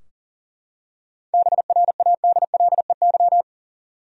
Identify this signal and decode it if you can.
Morse code, heard as a keyed tone, about 38 wpm, 700 Hz. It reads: BRADLEY
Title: USA Common Names - 40wpm